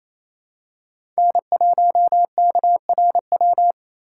Morse code, heard as a keyed tone, about 27 wpm, 700 Hz. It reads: N1KRW
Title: USA Call Signs - 3x Character Spacing - 28wpm